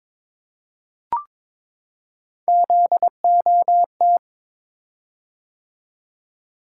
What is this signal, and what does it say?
Morse code, heard as a keyed tone, about 22 wpm, 700 Hz. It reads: ZOT
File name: practice / 22wpm / Three Letter-Number Combinations - 22wpm